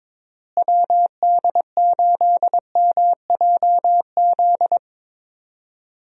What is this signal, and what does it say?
Morse code, heard as a keyed tone, about 22 wpm, 700 Hz. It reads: WD8MJZ
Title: USA Call Signs - 6 Characters - Repeated 3x - 22wpm